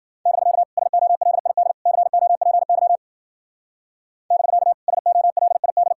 Morse code, heard as a keyed tone, about 50 wpm, 700 Hz. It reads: <BK> SOLID COPY <BK> SOLID